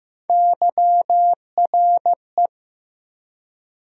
Morse code, heard as a keyed tone, about 15 wpm, 700 Hz. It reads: YRE
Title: USA Call Sign Postfixes - 3 Letters 15wpm